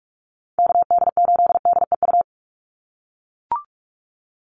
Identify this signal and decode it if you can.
Morse code, heard as a keyed tone, about 44 wpm, 700 Hz. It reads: KD8DEU
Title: NA SOTA Chaser Callsigns 45wpm